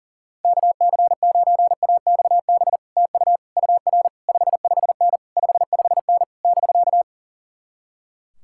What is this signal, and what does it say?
Morse code, heard as a keyed tone, about 52 wpm, 700 Hz. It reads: KC9AXB TU UR 55N 55N <BK>